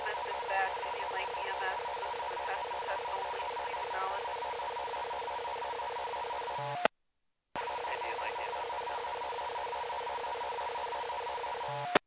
Two-tone teleprinter signal, about 450 Hz shift.
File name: Indian Lake EMS 2024 04 02 19 04 05